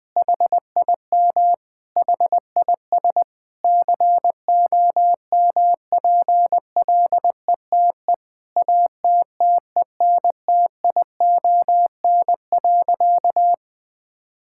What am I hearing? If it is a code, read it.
Morse code, heard as a keyed tone, about 20 wpm, 700 Hz. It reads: HIM HIS COMPLETE ATTENTION.
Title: Sentences from Top 1000 Words - Review Part 6 - 20wpm